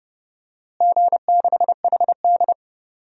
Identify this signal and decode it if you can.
Morse code, heard as a keyed tone, about 30 wpm, 700 Hz. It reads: G6HD
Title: DXCC-100 Call Signs - 4 Characters - Repeated 2x - 30wpm